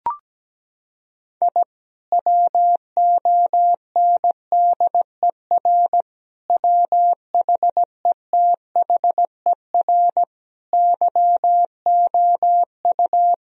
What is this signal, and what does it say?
Morse code, heard as a keyed tone, about 17 wpm, 700 Hz. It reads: I WONDER WHETHER YOU
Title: Sentences from Top 1000 Words - Review Part 3 - 17wpm